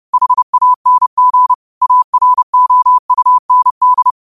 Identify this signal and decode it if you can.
Morse code, heard as a keyed tone, about 30 wpm, 1000 Hz. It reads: HANG AROUND
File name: qotdfiles / 30wpm